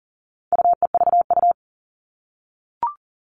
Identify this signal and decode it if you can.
Morse code, heard as a keyed tone, about 40 wpm, 700 Hz. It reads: UEVU
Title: All Cut Numbers - 4-Digit 40wpm